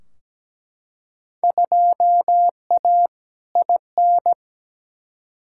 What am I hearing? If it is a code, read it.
Morse code, heard as a keyed tone, about 17 wpm, 700 Hz. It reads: 2A IN
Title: Field Day Exchanges - 17wpm